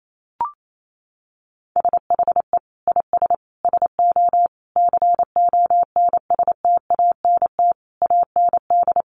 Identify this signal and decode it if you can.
Morse code, heard as a keyed tone, about 28 wpm, 700 Hz. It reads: SHE IS SO CONSTANT AND
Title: Sentences from Top 1500 Words 28wpm